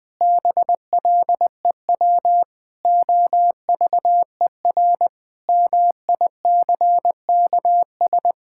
Morse code, heard as a keyed tone, about 20 wpm, 700 Hz. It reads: BLEW OVER MICKS
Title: Pangram Challenge v2 - 20wpm